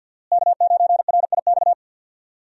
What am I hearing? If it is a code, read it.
Morse code, heard as a keyed tone, about 51 wpm, 700 Hz. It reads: K9RIX
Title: USA Call Signs - 5 Characters - Repeated 2x - 50wpm